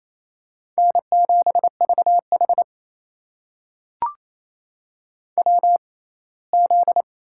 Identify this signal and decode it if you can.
Morse code, heard as a keyed tone, about 28 wpm, spaced as for 19 wpm, 700 Hz. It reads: N7VH WZ
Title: USA Call Signs - 4 Characters - 6x Character Spacing - 28wpm